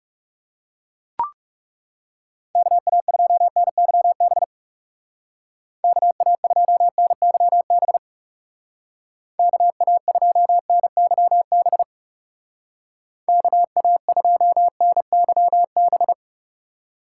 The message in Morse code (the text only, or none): KA2NYB KA2NYB KA2NYB KA2NYB